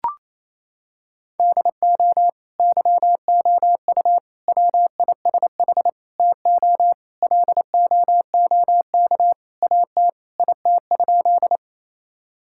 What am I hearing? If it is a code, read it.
Morse code, heard as a keyed tone, about 28 wpm, 700 Hz. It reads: DO YOU WISH TO LOOK AT IT?
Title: Sentences from Top 1600 Words 28wpm